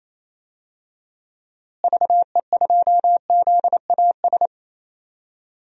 Morse code, heard as a keyed tone, about 28 wpm, 700 Hz. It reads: VE2ZAS
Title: DXCC-100 Call Signs - 6 Characters - Repeated 2x - 28wpm